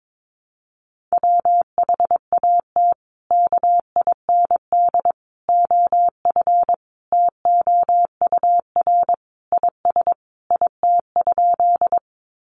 Morse code, heard as a keyed tone, about 22 wpm, 700 Hz. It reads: WHAT KIND OF TOUR IS IT?
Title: Sentences from Top 1400 Words 22wpm